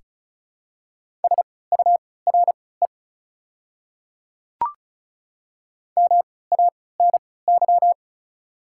Morse code, heard as a keyed tone, about 36 wpm, spaced as for 20 wpm, 690 Hz. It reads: SURE MANY
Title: 3x Character Spacing - Top 500 Words - 35wpm